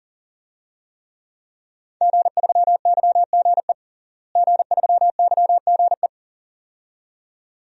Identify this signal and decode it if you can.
Morse code, heard as a keyed tone, about 40 wpm, 690 Hz. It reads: G3YGE G3YGE
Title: DXCC-100 Call Signs - 5 Characters - Repeated 2x - 40wpm